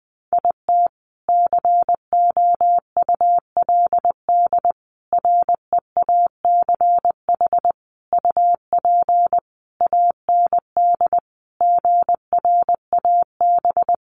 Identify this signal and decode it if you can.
Morse code, heard as a keyed tone, about 20 wpm, 700 Hz. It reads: IT COULD REACH UP AND GRAB
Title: A Sound Of Thunder 20wpm